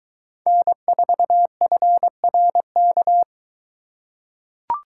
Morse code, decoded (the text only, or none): N4FRK